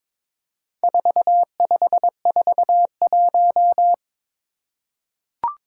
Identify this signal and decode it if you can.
Morse code, heard as a keyed tone, about 22 wpm, 700 Hz. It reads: 4541
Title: Four Challenging Letter-Number Combinations 22wpm